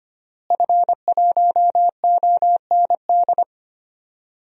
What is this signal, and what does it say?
Morse code, heard as a keyed tone, about 25 wpm, 700 Hz. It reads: F1OND
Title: DXCC-100 Call Signs - 5 Characters - Repeated 2x - 25wpm